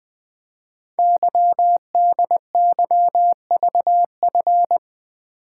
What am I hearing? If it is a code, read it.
Morse code, heard as a keyed tone, about 20 wpm, 700 Hz. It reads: YDYVF
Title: Code Groups - 5 Letters - 20wpm